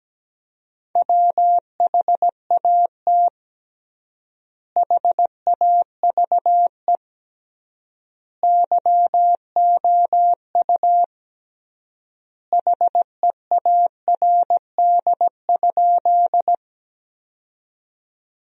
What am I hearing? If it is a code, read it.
Morse code, heard as a keyed tone, about 17 wpm, 700 Hz. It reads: WHAT HAVE YOU HEARD?
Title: 3x Word-Spacing - Sentences from Top 400 Words - 17wpm